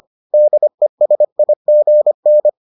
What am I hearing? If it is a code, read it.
Morse code, heard as a keyed tone, about 25 wpm, 600 Hz. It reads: DESIGN